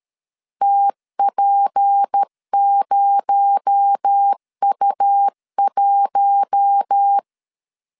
Morse code, heard as a keyed tone, about 13 wpm, 800 Hz. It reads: TP0U1
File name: CW / DET / 1 6 1500C S B 12wpm 3431